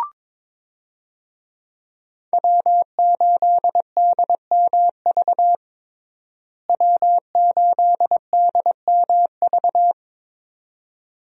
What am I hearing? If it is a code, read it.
Morse code, heard as a keyed tone, about 22 wpm, 700 Hz. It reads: W8DMV W8DMV
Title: USA Call Signs - 5 Characters - Repeated 2x - 22wpm